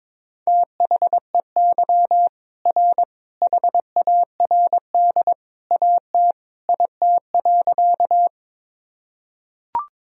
Morse code, heard as a keyed tone, about 21 wpm, 700 Hz. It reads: THEY R HARD AT IT.
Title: Sentences from Top 300 Words and Abbreviations 22wpm